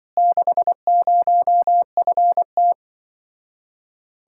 Morse code, heard as a keyed tone, about 23 wpm, 700 Hz. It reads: 60FT
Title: Speed-Racing Common QSO Phrases Encore - 20wpm